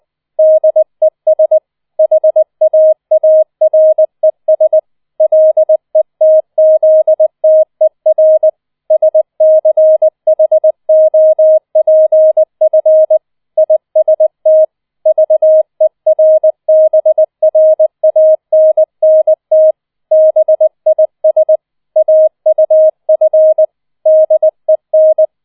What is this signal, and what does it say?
Morse code, heard as a keyed tone, about 20 wpm, 630 Hz. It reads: DES HAARES LETZTER SCHOPF IST VERBRANNT BIS AUF DEN